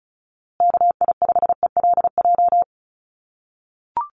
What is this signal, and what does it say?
Morse code, heard as a keyed tone, about 34 wpm, 700 Hz. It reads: KI5ELJ
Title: USA Call Signs Repeated 2x 35wpm